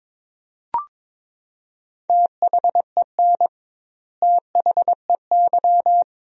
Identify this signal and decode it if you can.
Morse code, heard as a keyed tone, about 22 wpm, 700 Hz. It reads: THEN THEY
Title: 2x Word-Spacing - Sentences from Top 200 Words - 22wpm